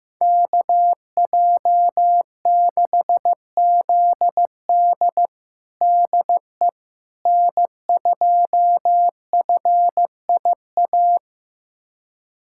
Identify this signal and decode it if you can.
Morse code, heard as a keyed tone, about 15 wpm, 700 Hz. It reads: KJ6ZD DE N2FIA
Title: USA Callsign DE Callsign - 5 Characters - 15wpm